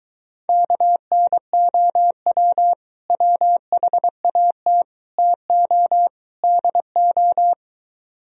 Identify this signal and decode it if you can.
Morse code, heard as a keyed tone, about 23 wpm, 700 Hz. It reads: KNOW WHAT TO DO